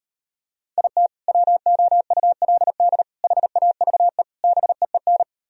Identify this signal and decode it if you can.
Morse code, heard as a keyed tone, about 38 wpm, 700 Hz. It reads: IT WOULD HAVE BEEN